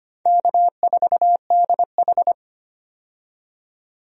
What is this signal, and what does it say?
Morse code, heard as a keyed tone, about 25 wpm, 700 Hz. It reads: K4DH
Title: USA Call Signs - 4 Characters - 25wpm